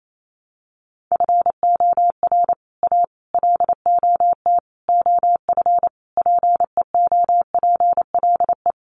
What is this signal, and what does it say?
Morse code, heard as a keyed tone, about 28 wpm, 700 Hz. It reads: FOR A LOT OF PEOPLE
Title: Sets of 5 Words Encore 28wpm